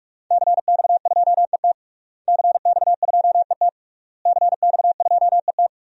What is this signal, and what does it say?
Morse code, heard as a keyed tone, about 46 wpm, 700 Hz. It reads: CX2ET CX2ET CX2ET
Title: DXCC-100 Call Signs - 5 Characters - Repeated 3x - 45wpm